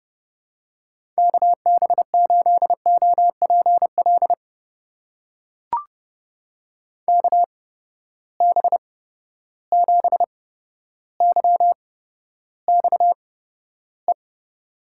Morse code, heard as a keyed tone, about 31 wpm, spaced as for 13 wpm, 700 Hz. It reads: KB8OPL KB7YXE
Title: USA Call Signs - 8x Character Spacing - 30wpm